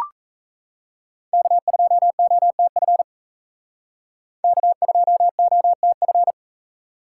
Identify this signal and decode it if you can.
Morse code, heard as a keyed tone, about 36 wpm, 700 Hz. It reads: K2OTF K2OTF